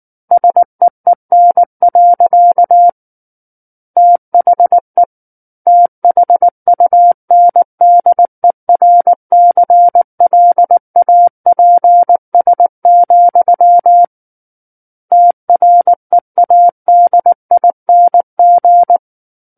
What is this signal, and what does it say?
Morse code, heard as a keyed tone, about 19 wpm, 700 Hz. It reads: SEEN. THE THUNDERCLAPS, TREADING
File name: Book One / 10-Chapter 10 - 19 WPM